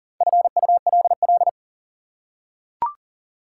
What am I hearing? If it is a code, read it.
Morse code, heard as a keyed tone, about 40 wpm, 700 Hz. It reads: FULL
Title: Four Letter Words - 40wpm